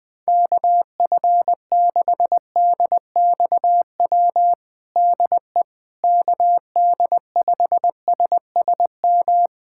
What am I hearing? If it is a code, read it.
Morse code, heard as a keyed tone, about 20 wpm, 700 Hz. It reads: KF6DXW DE KD5SSM
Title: USA Callsign DE Callsign - 6 Characters - 20wpm